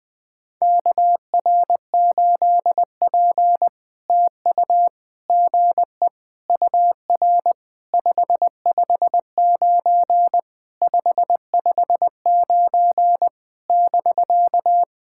Morse code, heard as a keyed tone, about 19 wpm, 700 Hz. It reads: KR8P TU GE UR 559 559 <BK>
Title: POTA QSO Practice - 20wpm